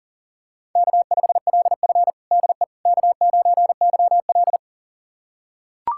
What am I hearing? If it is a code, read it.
Morse code, heard as a keyed tone, about 40 wpm, 700 Hz. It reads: K5LF DE K9YL
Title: USA Callsign DE Callsign - 4 Characters - 40wpm